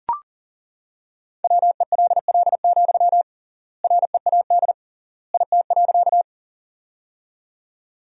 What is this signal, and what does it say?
Morse code, heard as a keyed tone, about 40 wpm, 700 Hz. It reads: WELL, READ IT.
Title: 3x Word-Spacing - Sentences from Top 200 Words - 40wpm